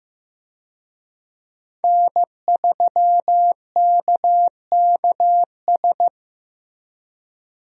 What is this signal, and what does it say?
Morse code, heard as a keyed tone, about 15 wpm, 700 Hz. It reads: N3KKS
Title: USA Call Signs - 5 Characters - Repeated 3x - 15wpm